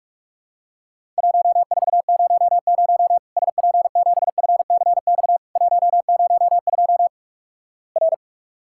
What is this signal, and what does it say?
Morse code, heard as a keyed tone, about 44 wpm, 700 Hz. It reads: 1400 SP7FCX 102 R
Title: Worked All Europe DX Contest - QTC Exchange - 45wpm